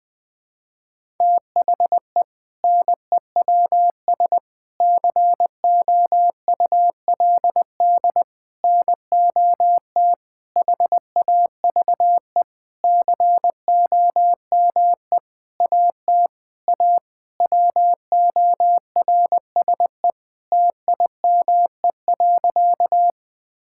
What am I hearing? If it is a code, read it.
Morse code, heard as a keyed tone, about 21 wpm, 700 Hz. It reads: THE NEWS COULD NOT HAVE COME AT A WORSE TIME.
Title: Sentences with 8-Word Phrases - 20wpm